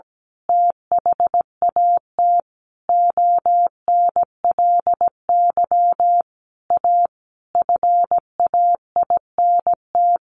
Morse code, heard as a keyed tone, about 17 wpm, 700 Hz. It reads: THAT ONLY A FAINT
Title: A Sound Of Thunder 17wpm